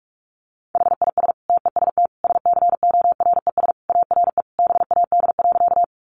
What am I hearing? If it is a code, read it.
Morse code, heard as a keyed tone, about 44 wpm, 700 Hz. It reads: HIS TEST SCORES ARE BAD.
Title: Additional Easy Phrases 45wpm